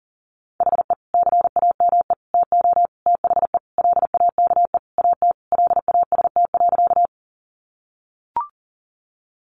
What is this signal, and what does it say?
Morse code, heard as a keyed tone, about 40 wpm, 700 Hz. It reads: HE CAME TO THE LAKE AT LAST.
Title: Sentences from Top 1100 Words 40wpm